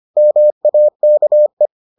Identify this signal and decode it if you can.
Morse code, heard as a keyed tone, about 25 wpm, 600 Hz. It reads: MAKE